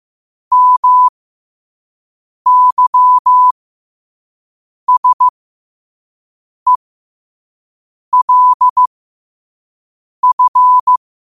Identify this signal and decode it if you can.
Morse code, heard as a keyed tone, about 15 wpm, spaced as for 5 wpm, 1000 Hz. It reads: MYSELF
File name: qotdfiles / 5wpm